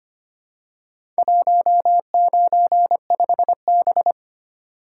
Morse code, heard as a keyed tone, about 25 wpm, 690 Hz. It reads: 195B
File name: practice / 25wpm / Four Letter-Number Combinations - 25wpm